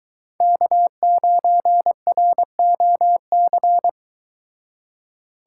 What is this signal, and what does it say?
Morse code, heard as a keyed tone, about 22 wpm, 700 Hz. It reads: K9ROC